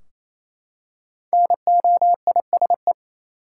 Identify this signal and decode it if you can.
Morse code, heard as a keyed tone, about 28 wpm, 700 Hz. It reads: NOISE
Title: Five Letter Words - 28wpm